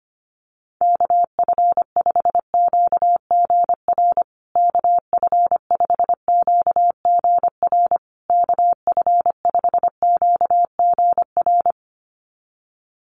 Morse code, heard as a keyed tone, about 25 wpm, 700 Hz. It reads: KF5QGR KF5QGR KF5QGR
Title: USA Call Signs Repeated 3x 25wpm